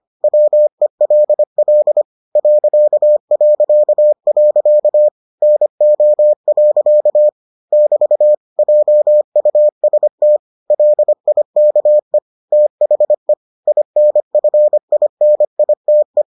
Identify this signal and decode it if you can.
Morse code, heard as a keyed tone, about 25 wpm, 600 Hz. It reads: WELL ... NO. <BT> JUST LIKE THE INFINITE